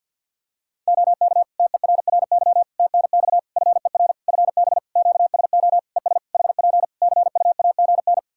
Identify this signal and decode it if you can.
Morse code, heard as a keyed tone, about 50 wpm, 700 Hz. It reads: OK TERRY TNX FER FB QSO ES HP CUAGN
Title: Practice QSO - Exchange 4 - Ending - 50wpm